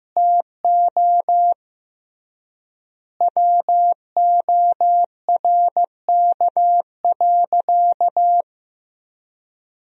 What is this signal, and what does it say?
Morse code, heard as a keyed tone, about 16 wpm, 700 Hz. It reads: TO WORK.
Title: 3x Word-Spacing - Sentences from Top 300 Words - 15wpm